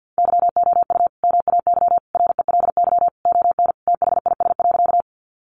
Morse code, heard as a keyed tone, about 50 wpm, 700 Hz. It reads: YOU MAY RELY ON THIS.
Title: Sentences from Top 1200 Words 50wpm